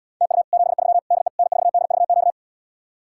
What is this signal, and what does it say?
Morse code, heard as a keyed tone, about 77 wpm, 700 Hz. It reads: TU 73 DE N4DVQ